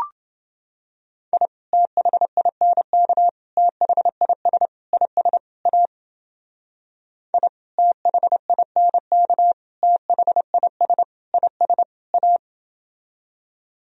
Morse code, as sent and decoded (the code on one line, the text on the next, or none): .. / - .... .. -. -.- / - .... .. ... / .. ... / .- / .. / - .... .. -. -.- / - .... .. ... / .. ... / .-
I THINK THIS IS A I THINK THIS IS A